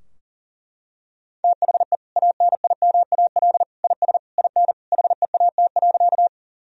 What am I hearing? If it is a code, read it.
Morse code, heard as a keyed tone, about 37 wpm, 700 Hz. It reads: THE ANIMAL IS IN HEAT.
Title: Sentences from Top 1000 Words - Review Part 3 - 40wpm